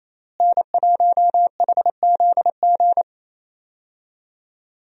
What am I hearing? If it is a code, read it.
Morse code, heard as a keyed tone, about 28 wpm, 700 Hz. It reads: N1HZG
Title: USA Call Signs - 5 Characters - Repeated 3x - 28wpm